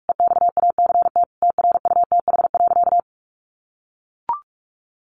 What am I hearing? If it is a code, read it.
Morse code, heard as a keyed tone, about 44 wpm, 700 Hz. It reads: EXACT TRUTH.
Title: Sentences from Top 1300 Words 45wpm